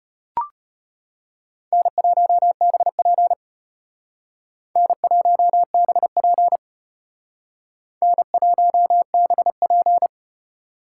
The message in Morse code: N1BP N1BP N1BP

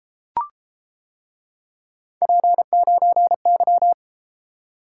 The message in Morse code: P9Y

